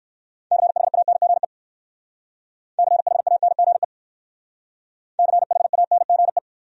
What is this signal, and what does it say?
Morse code, heard as a keyed tone, about 56 wpm, 700 Hz. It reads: CHANGE CHANGE CHANGE
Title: Speed-Racing Top 500 Words 45wpm